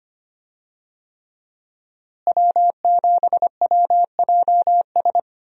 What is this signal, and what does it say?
Morse code, heard as a keyed tone, about 25 wpm, 700 Hz. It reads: W7WJS
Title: USA Call Signs - 5 Characters - Repeated 3x - 25wpm